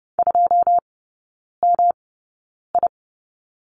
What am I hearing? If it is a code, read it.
Morse code, heard as a keyed tone, about 30 wpm, spaced as for 9 wpm, 700 Hz. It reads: 2MI
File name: Lesson 18e - Letter m - Three Characters 30wpm